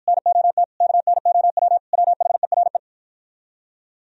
Morse code, heard as a keyed tone, about 52 wpm, 700 Hz. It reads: NOT KNOW WHERE